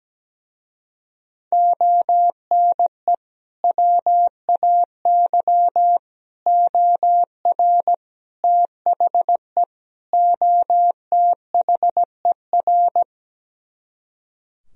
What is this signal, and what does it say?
Morse code, heard as a keyed tone, about 17 wpm, 700 Hz. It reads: ONE WAY OR THE OTHER
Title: Speed-Racing - Sets of 5 Words 17wpm